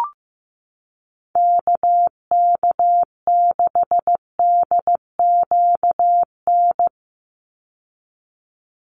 Morse code, heard as a keyed tone, about 15 wpm, 700 Hz. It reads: KK6DQN
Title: USA Call Signs 15wpm